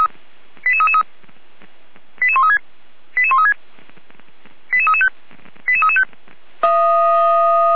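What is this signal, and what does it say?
Five-tone selective call (ZVEI), heard as quick runs of five tones, tones of about 70 ms. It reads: 88333 88136 88136 88335 88335
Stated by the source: ZVEI five-tone selective call received with an SDR.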